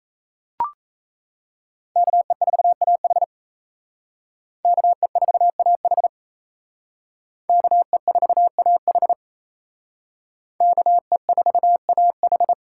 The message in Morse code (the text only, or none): KE4AH KE4AH KE4AH KE4AH